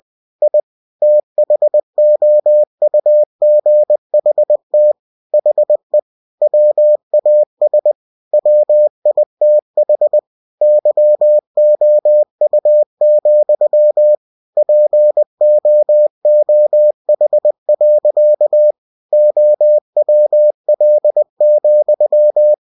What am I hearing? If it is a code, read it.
Morse code, heard as a keyed tone, about 20 wpm, 600 Hz. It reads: I THOUGHT HE WAS WITH YOU, POOH. OWL,
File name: cw-books / winnie-the-pooh / winnie-20wpm-page-0098